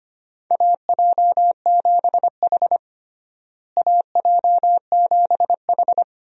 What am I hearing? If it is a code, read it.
Morse code, heard as a keyed tone, about 25 wpm, 700 Hz. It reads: AJ7H AJ7H
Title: USA Call Signs - 4 Characters - Repeated 2x - 25wpm